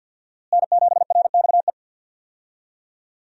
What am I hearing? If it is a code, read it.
Morse code, heard as a keyed tone, about 51 wpm, 700 Hz. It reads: N7RXE
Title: USA Call Signs - 5 Characters - Repeated 2x - 50wpm